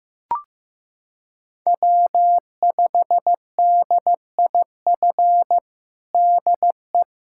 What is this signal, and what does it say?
Morse code, heard as a keyed tone, about 15 wpm, 700 Hz. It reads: W5DIF DE
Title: USA Callsign DE Callsign 15wpm